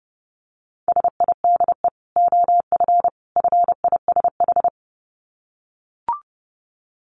Morse code, heard as a keyed tone, about 30 wpm, 700 Hz. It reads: SIDE OF FISH
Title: Dit Challenge 30wpm